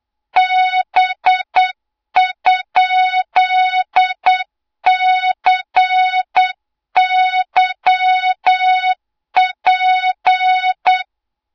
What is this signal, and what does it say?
Morse code, heard as a keyed tone, about 8 wpm, 3700 Hz, with noise. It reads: B?CYP
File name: Morze-HA2MN / hang / 20 GYAKORLAT